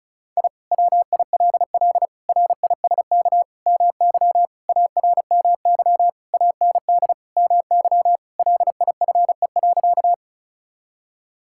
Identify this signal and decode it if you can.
Morse code, heard as a keyed tone, about 36 wpm, 700 Hz. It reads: I WILL RISK MY ARMY AND MY LIFE.
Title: Sentences from Top 1000 Words - Review Part 5 - 35wpm